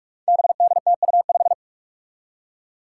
Morse code, heard as a keyed tone, about 46 wpm, 700 Hz. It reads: BDTU5
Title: Five Challenging Letter-Number Combinations 45wpm